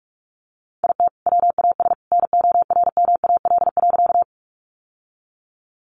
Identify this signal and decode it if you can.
Morse code, heard as a keyed tone, about 44 wpm, 700 Hz. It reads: IT WAS NORMAL.
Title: Sentences from Top 1200 Words 45wpm